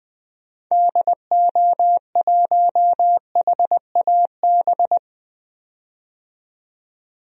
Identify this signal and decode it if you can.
Morse code, heard as a keyed tone, about 20 wpm, 700 Hz. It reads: DO1HAB
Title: DXCC-100 Call Signs - 6 Characters - 20wpm